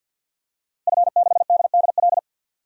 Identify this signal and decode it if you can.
Morse code, heard as a keyed tone, about 47 wpm, 700 Hz. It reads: F6DDL
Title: DXCC-100 Call Signs - 5 Characters - Repeated 3x - 50wpm